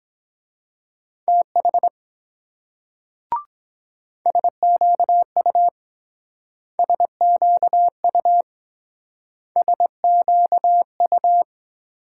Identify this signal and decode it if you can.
Morse code, heard as a keyed tone, about 23 wpm, 700 Hz. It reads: TH SQU SQU SQU